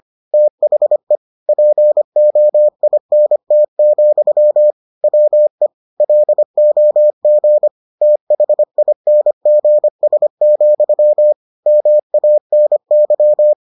Morse code, heard as a keyed tone, about 25 wpm, 600 Hz. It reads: THE POINT, WE LOG THINGS, MANY